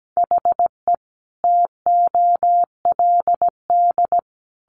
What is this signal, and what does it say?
Morse code, heard as a keyed tone, about 17 wpm, 700 Hz. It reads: HE TOLD
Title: Sentences from Top 700 Words 17wpm